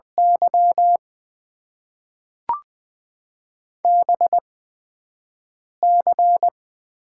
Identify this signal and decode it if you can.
Morse code, heard as a keyed tone, about 20 wpm, spaced as for 5 wpm, 700 Hz. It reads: Y BC
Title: Top 90 QSO Elements - 8x Character Spacing - 20wpm